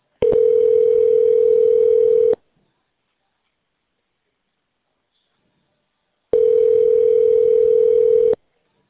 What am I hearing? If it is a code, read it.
Ringback tone (North America), 440 + 480 Hz.